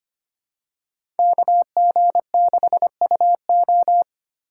Morse code, heard as a keyed tone, about 25 wpm, 700 Hz. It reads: KG6UO
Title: USA Call Signs - 5 Characters - 4x Character Spacing - 25wpm